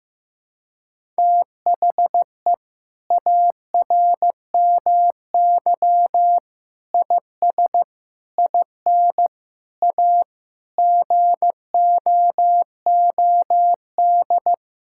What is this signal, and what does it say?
Morse code, heard as a keyed tone, about 15 wpm, 700 Hz. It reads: THE ARMY IS IN A GOOD
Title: Sentences from Top 1000 Words - Review Part 6 - 15wpm